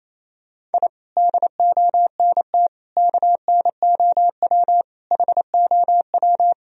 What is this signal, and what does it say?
Morse code, heard as a keyed tone, about 28 wpm, 700 Hz. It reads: I DONT KNOW HOW
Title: Sentences from Top 1000 Words - Review Part 2 - 28wpm